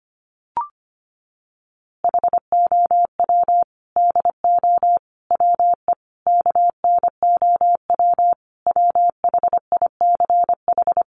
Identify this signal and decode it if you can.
Morse code, heard as a keyed tone, about 25 wpm, 700 Hz. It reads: HOW DO WE KNOW WHICH
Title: A Sound Of Thunder 25wpm